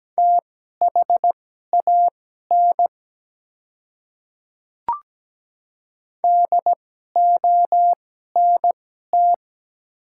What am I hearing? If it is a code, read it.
Morse code, heard as a keyed tone, about 17 wpm, spaced as for 12 wpm, 690 Hz. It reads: THAN DONT
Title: 2x Character Spacing - Top 300 Words - 17wpm